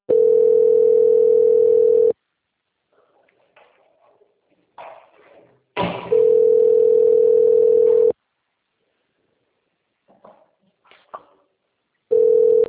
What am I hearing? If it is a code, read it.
Ringback tone (North America), 440 + 480 Hz.
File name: out- - - - - .9333